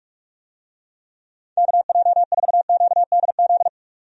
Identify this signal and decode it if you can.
Morse code, heard as a keyed tone, about 46 wpm, 700 Hz. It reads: KJ4QDZ
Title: USA Call Signs - 6 Characters - 45wpm